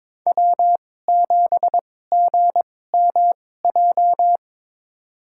Morse code, heard as a keyed tone, about 22 wpm, spaced as for 16 wpm, 700 Hz. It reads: W7GMJ
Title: USA Call Signs - 5 Characters - 2x Character Spacing - 22wpm